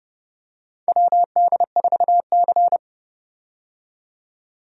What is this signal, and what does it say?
Morse code, heard as a keyed tone, about 30 wpm, 700 Hz. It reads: WD4C
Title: USA Call Signs - 4 Characters - 30wpm